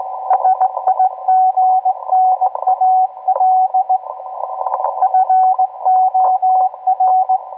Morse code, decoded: SIGNAL FOR